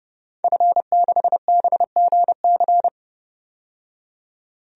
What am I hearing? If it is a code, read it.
Morse code, heard as a keyed tone, about 30 wpm, 700 Hz. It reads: F6BGC
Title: DXCC-100 Call Signs - 5 Characters - 30wpm